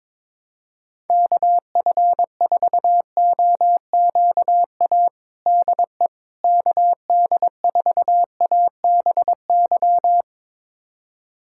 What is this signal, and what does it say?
Morse code, heard as a keyed tone, about 22 wpm, 700 Hz. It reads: KF4OQA DE KD4ABY
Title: USA Callsign DE Callsign 22wpm